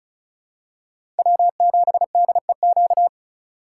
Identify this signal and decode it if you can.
Morse code, heard as a keyed tone, about 34 wpm, 700 Hz. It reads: W7DEQ
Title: USA Call Signs - 5 Characters - 3x Character Spacing - 35wpm